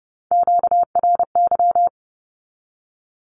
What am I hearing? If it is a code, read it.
Morse code, heard as a keyed tone, about 30 wpm, 700 Hz. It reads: QRY
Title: Q-Codes 30wpm